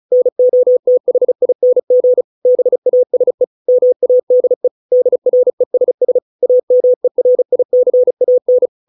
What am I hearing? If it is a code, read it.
Morse code, heard as a keyed tone, about 34 wpm, 500 Hz. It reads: NOTHING BASE MADE DRESS AMERICAN